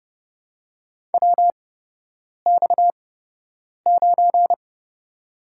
Morse code, heard as a keyed tone, about 30 wpm, spaced as for 8 wpm, 700 Hz. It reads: WX9
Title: USA Call Signs - 5 Characters - 8x Character Spacing - 30wpm